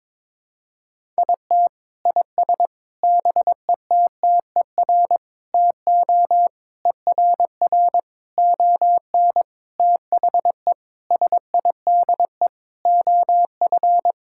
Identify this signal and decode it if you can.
Morse code, heard as a keyed tone, about 22 wpm, 700 Hz. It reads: IT IS BETTER TO ERR ON THE SIDE OF
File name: Sentences with 7-Word Phrases - 22wpm